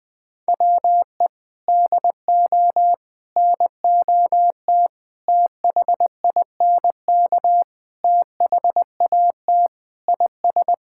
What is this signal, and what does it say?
Morse code, heard as a keyed tone, about 20 wpm, 700 Hz. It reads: WE DO NOT THINK THAT IS
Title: Sentences with 4-Word Phrases - 20wpm